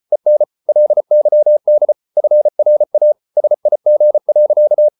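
Morse code, heard as a keyed tone, about 34 wpm, 600 Hz. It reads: EN LYD FRA SIG.
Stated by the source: Morse code at 33 wpm, QRQ.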